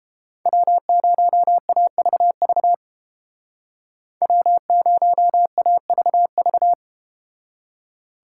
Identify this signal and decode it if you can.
Morse code, heard as a keyed tone, about 31 wpm, 700 Hz. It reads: W0AVV W0AVV